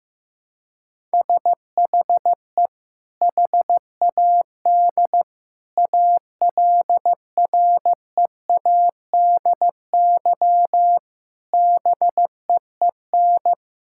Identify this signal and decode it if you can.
Morse code, heard as a keyed tone, about 16 wpm, 700 Hz. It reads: SHE HAD ALREADY BEEN
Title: Sentences from Top 1000 Words - Review Part 6 - 15wpm